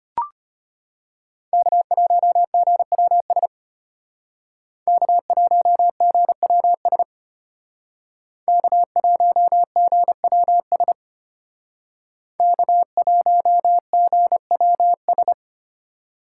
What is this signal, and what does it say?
Morse code, heard as a keyed tone, about 31 wpm, 700 Hz. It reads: K1GWS K1GWS K1GWS K1GWS
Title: Speed Racing - USA 1x3 Call Signs 25wpm